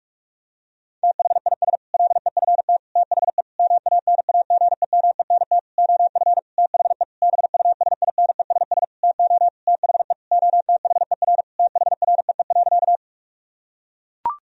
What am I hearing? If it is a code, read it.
Morse code, heard as a keyed tone, about 41 wpm, 700 Hz. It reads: THIS LEFT THE MANAGEMENT OF THE BUSINESS TO THE OTHER THREE.
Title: Sentences from Top 1000 Words - Review Part 4 - 45wpm